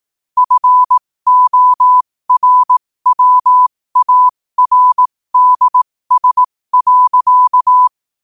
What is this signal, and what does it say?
Morse code, heard as a keyed tone, about 18 wpm, spaced as for 15 wpm, 1000 Hz. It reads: FORWARDS.
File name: qotdfiles / 15wpm